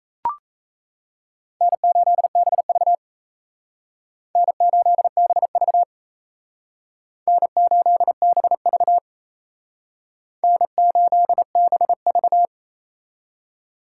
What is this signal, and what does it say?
Morse code, heard as a keyed tone, about 35 wpm, 700 Hz. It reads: N8BV N8BV N8BV N8BV